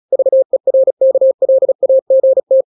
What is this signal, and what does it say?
Morse code, heard as a keyed tone, about 36 wpm, 530 Hz. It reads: VERKLAGT